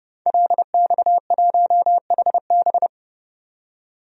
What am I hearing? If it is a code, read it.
Morse code, heard as a keyed tone, about 31 wpm, 700 Hz. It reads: LX1HB
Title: DXCC-100 Call Signs - 5 Characters - 30wpm